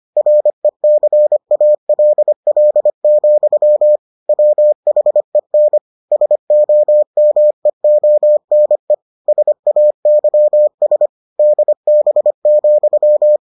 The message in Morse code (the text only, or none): RECALL, WHEN SOMEONE SAYS DB,